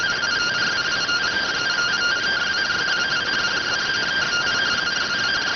Two-tone teleprinter signal, about 170 Hz shift.